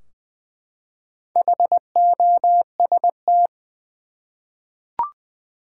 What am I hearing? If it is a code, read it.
Morse code, heard as a keyed tone, about 20 wpm, 700 Hz. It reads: HOST
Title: Four Letter Words - 20wpm